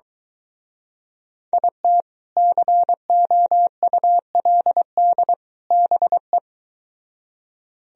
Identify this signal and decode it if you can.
Morse code, heard as a keyed tone, about 23 wpm, 700 Hz. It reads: IT COULD BE